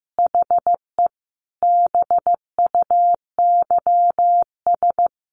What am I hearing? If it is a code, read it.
Morse code, heard as a keyed tone, about 15 wpm, 700 Hz. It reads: HE BUYS
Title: Additional Easy Phrases 15wpm